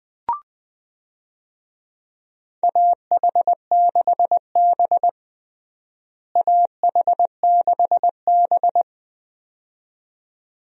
Morse code, heard as a keyed tone, about 20 wpm, 700 Hz. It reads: AH6B AH6B
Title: USA Call Signs - 4 Characters - Repeated 2x - 20wpm